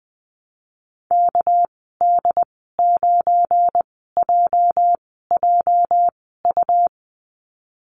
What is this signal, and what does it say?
Morse code, heard as a keyed tone, about 20 wpm, spaced as for 15 wpm, 700 Hz. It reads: KD9JJU
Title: 207 Lesson 33g - Letter j - Call Signs 20wpm